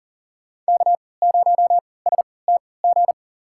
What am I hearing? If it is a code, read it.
Morse code, heard as a keyed tone, about 38 wpm, spaced as for 22 wpm, 700 Hz. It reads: K0STG
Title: USA Call Signs - 5 Characters - 3x Character Spacing - 40wpm